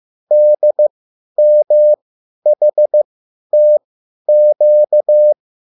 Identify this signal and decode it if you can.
Morse code, heard as a keyed tone, about 15 wpm, spaced as for 10 wpm, 600 Hz. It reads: DMHTQ
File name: QSL / CW lesson